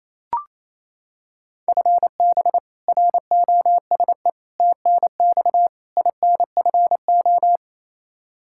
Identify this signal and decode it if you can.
Morse code, heard as a keyed tone, about 28 wpm, 700 Hz. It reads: FB ROSE TNX INFO
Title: Practice QSO - Exchange 3 - More Info - 28wpm